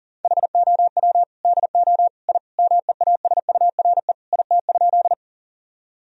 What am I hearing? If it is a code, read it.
Morse code, heard as a keyed tone, about 40 wpm, 700 Hz. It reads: HOW DO I MEASURE IT?
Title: Sentences from Top 1000 Words - Review Part 2 - 40wpm